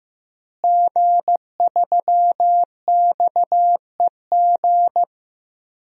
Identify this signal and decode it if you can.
Morse code, heard as a keyed tone, about 15 wpm, 700 Hz. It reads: G3XEG
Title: DXCC-100 Call Signs - 5 Characters - Repeated 3x - 15wpm